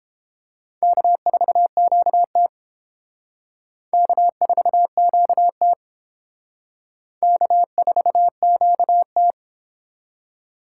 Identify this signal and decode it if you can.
Morse code, heard as a keyed tone, about 30 wpm, 700 Hz. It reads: K4QT K4QT K4QT